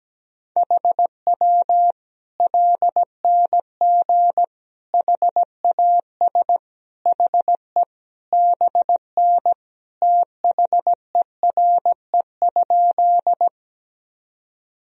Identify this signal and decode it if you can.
Morse code, heard as a keyed tone, about 17 wpm, 700 Hz. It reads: HW LNG HAS HE BN THERE?
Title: Sentences from Top 100 Words and Abbreviations 17wpm